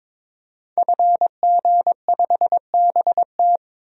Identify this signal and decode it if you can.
Morse code, heard as a keyed tone, about 22 wpm, 700 Hz. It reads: FG5BT
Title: DXCC-100 Call Signs - 5 Characters - 22wpm